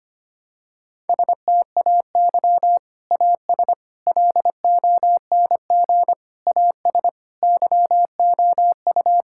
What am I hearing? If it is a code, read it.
Morse code, heard as a keyed tone, about 25 wpm, 700 Hz. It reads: STAY AS LONG AS YOU
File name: practice / 25wpm / Sentences with 3-Word Phrases - 25wpm